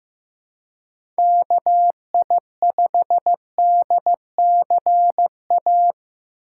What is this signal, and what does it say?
Morse code, heard as a keyed tone, about 15 wpm, 700 Hz. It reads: KI5DCA
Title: USA Call Signs - 4x Character Spacing - 15wpm